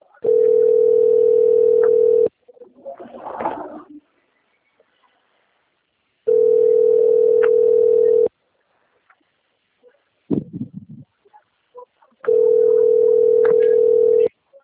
Ringback tone (North America), 440 + 480 Hz.
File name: rg-1600-unknown- - - .30104